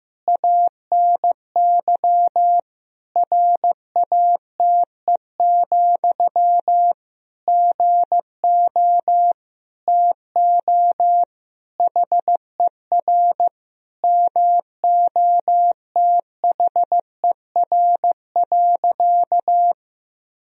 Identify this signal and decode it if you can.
Morse code, heard as a keyed tone, about 15 wpm, 700 Hz. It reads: ANY RATE, GO TO HER MOTHER.
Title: Sentences from Top 1000 Words - Review Part 4 - 15wpm